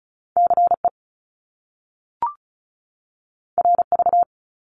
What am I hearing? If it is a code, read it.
Morse code, heard as a keyed tone, about 36 wpm, 700 Hz. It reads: CE RV
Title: Top Two Letter Combinations in Words 35wpm